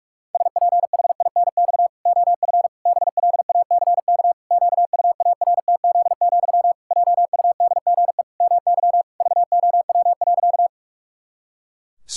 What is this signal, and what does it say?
Morse code, heard as a keyed tone, about 46 wpm, 700 Hz. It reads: SPHINX OF BLACK QUARTZ, JUDGE MY VOW.
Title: Pangram Challenge v2 - 45wpm